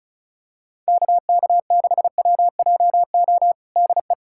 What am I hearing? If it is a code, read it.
Morse code, heard as a keyed tone, about 34 wpm, 700 Hz. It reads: KK6WJO DE
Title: USA Callsign DE Callsign 35wpm